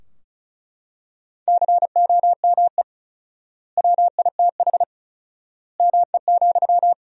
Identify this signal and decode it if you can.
Morse code, heard as a keyed tone, about 36 wpm, 700 Hz. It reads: COME WITH ME,
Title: 4x Word-Spacing - Sentences from Top 200 Words - 35wpm